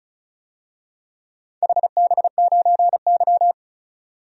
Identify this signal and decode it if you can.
Morse code, heard as a keyed tone, about 36 wpm, 700 Hz. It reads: HB9Y
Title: DXCC-100 Call Signs - 4 Characters - 35wpm